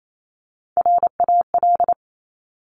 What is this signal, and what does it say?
Morse code, heard as a keyed tone, about 28 wpm, 700 Hz. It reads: RAL
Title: Top Three Letter Combinations in Words 28wpm